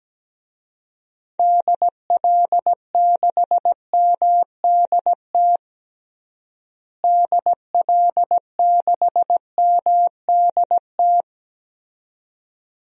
Morse code, heard as a keyed tone, about 17 wpm, 700 Hz. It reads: DL6MDT DL6MDT
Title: DXCC-100 Call Signs - 6 Characters - Repeated 2x - 17wpm